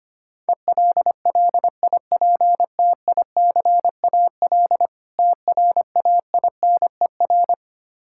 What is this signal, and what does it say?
Morse code, heard as a keyed tone, about 25 wpm, 700 Hz. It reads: ELLIPTICAL TRAINER
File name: Easy Words - Speed-Racing with Hints - 25wpm